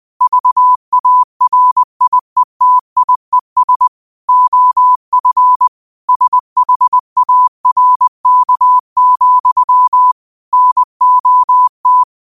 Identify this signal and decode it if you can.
Morse code, heard as a keyed tone, about 20 wpm, 1000 Hz. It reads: VARIETIES OF SHARK, NOT
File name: qotdfiles / 20wpm